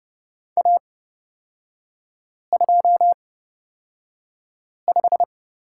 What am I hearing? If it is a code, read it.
Morse code, heard as a keyed tone, about 29 wpm, spaced as for 5 wpm, 700 Hz. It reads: A25
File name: Lesson 38e - Letter q - Three Characters 30wpm